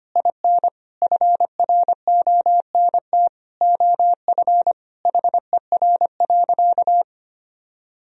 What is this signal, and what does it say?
Morse code, heard as a keyed tone, about 25 wpm, 700 Hz. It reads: IN FRONT OF HER.
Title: Sentences from Top 1000 Words - Review Part 2 - 25wpm